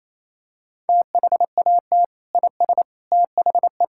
Morse code, heard as a keyed tone, about 28 wpm, 700 Hz. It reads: THAT IS THE
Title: Sentences from Top 1600 Words 28wpm